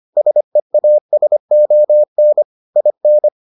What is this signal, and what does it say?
Morse code, heard as a keyed tone, about 25 wpm, 600 Hz. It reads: SEASON IN